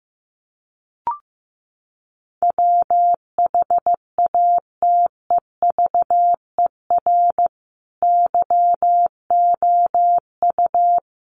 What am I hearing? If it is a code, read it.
Morse code, heard as a keyed tone, about 15 wpm, 700 Hz. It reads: WHATEVER YOU
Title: Sentences from Top 1100 Words 15wpm